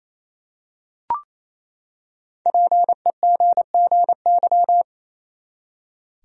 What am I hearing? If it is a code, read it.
Morse code, heard as a keyed tone, about 27 wpm, 700 Hz. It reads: PEGGY
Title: USA Common Names - 28wpm